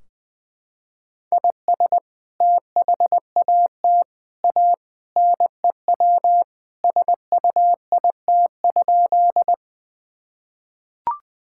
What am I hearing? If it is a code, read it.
Morse code, heard as a keyed tone, about 21 wpm, 700 Hz. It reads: IS THAT A NEW SUIT?
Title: Sentences from Top 1600 Words 20wpm